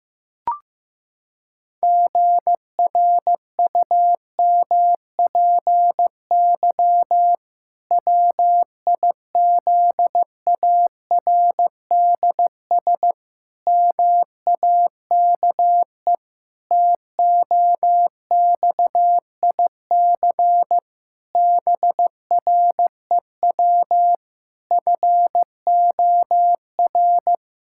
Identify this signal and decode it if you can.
Morse code, heard as a keyed tone, about 15 wpm, 700 Hz. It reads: GRUMPY WIZARDS MAKE TOXIC BREW FOR
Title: Pangram Challenge v2 - 15wpm